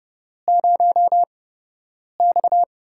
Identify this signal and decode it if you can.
Morse code, heard as a keyed tone, about 30 wpm, spaced as for 8 wpm, 700 Hz. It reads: 0X
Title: USA Call Signs - 4 Characters - 8x Character Spacing - 30wpm